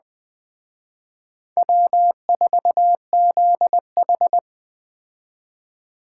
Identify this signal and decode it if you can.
Morse code, heard as a keyed tone, about 21 wpm, 700 Hz. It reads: W4ZH